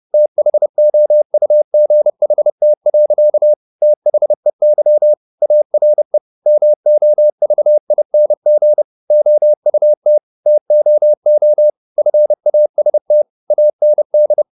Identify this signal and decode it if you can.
Morse code, heard as a keyed tone, about 30 wpm, 600 Hz. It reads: THOUGHT. THEY ARE MOVING OUT TOO FAST AND
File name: the old man and the sea / the old man and the sea 1 30wpm